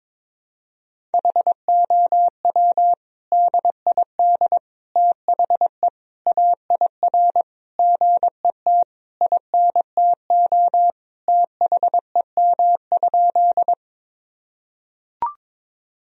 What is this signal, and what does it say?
Morse code, heard as a keyed tone, about 22 wpm, 700 Hz. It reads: HOW DID THE AIR GET INTO THEM?
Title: Sentences from Top 1000 Words - Review Part 1 - 22wpm